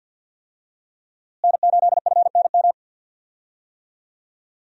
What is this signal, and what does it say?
Morse code, heard as a keyed tone, about 47 wpm, 700 Hz. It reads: N8FNM
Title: USA Call Signs - 5 Characters - Repeated 3x - 50wpm